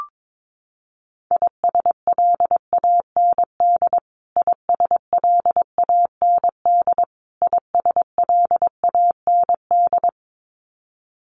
Morse code, heard as a keyed tone, about 22 wpm, 700 Hz. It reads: ISLAND ISLAND ISLAND
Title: Top 500 Words Repeated 22wpm